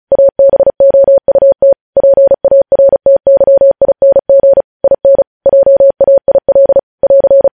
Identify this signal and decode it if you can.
Morse code, heard as a keyed tone, about 36 wpm, 550 Hz. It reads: ABOUT PARTYING IN JAIL <AR>